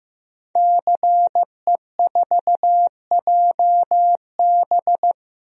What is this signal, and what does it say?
Morse code, heard as a keyed tone, about 15 wpm, 700 Hz. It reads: CE4JB
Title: DXCC-100 Call Signs - 5 Characters - 15wpm